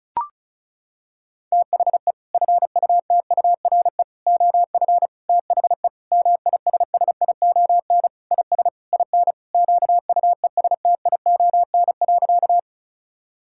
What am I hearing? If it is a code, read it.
Morse code, heard as a keyed tone, about 36 wpm, 700 Hz. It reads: THE FUTURE OF THE MISSION IS IN QUESTION.
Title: Sentences from Top 1600 Words 35wpm